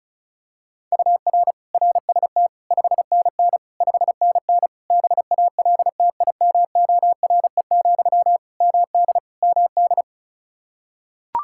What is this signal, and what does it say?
Morse code, heard as a keyed tone, about 34 wpm, 700 Hz. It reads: UR RST 5NN 5NN BALTIMORE, MD MD
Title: Practice QSO - Exchange 1 - Beginning - 35wpm